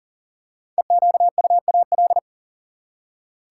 Morse code, heard as a keyed tone, about 40 wpm, 700 Hz. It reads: EQUAL
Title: Five Letter Words - 40wpm